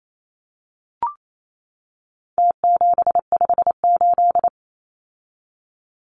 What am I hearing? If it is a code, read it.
Morse code, heard as a keyed tone, about 28 wpm, 700 Hz. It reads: T758
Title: Numbers - Contest Style - 4-Digit 28wpm